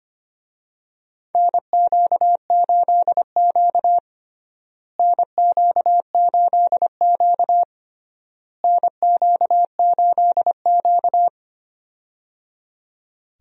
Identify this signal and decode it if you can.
Morse code, heard as a keyed tone, about 25 wpm, 700 Hz. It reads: NQ8Q NQ8Q NQ8Q
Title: USA Call Signs - 4 Characters - Repeated 3x - 25wpm